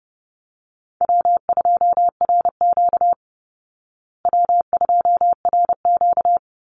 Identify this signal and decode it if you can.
Morse code, heard as a keyed tone, about 29 wpm, 700 Hz. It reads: W2RQ W2RQ
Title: Easy Call Signs Repeated 2x - 30wpm